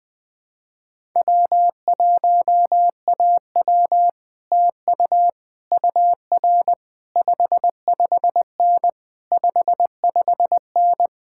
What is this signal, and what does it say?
Morse code, heard as a keyed tone, about 20 wpm, 700 Hz. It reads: W1AW TU UR 55N 55N
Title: Speed-Racing Common QSO Phrases - 20wpm